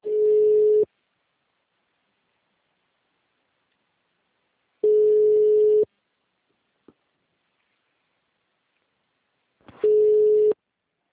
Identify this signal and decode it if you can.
Ringback tone (Europe, 425 Hz).